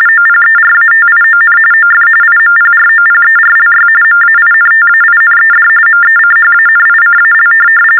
TTY/TDD text telephone tones, about 45.45 baud. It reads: OPERATION OF IMPROPERLY MAINTAINED MECHANICAL
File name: teletype-RTTY-45